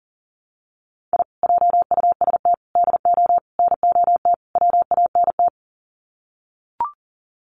Morse code, heard as a keyed tone, about 40 wpm, 700 Hz. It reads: I JUST DO NOT WANT
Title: Sets of 5 Words Encore2 40wpm